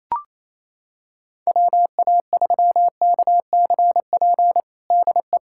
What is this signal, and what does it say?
Morse code, heard as a keyed tone, about 28 wpm, 700 Hz. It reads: WA3KCP DE
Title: USA Callsign DE Callsign - 6 Characters - 28wpm